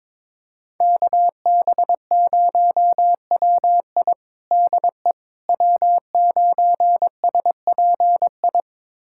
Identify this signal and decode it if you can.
Morse code, heard as a keyed tone, about 22 wpm, 700 Hz. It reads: KB0WI DE W9SPI
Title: USA Callsign DE Callsign - 5 Characters - 22wpm